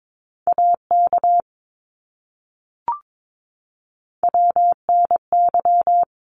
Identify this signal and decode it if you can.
Morse code, heard as a keyed tone, about 22 wpm, 700 Hz. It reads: AK WNY
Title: ARRL Sections 22wpm